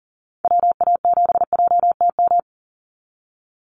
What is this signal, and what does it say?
Morse code, heard as a keyed tone, about 40 wpm, 700 Hz. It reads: WA7JTM
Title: NA SOTA Activator Callsigns 40wpm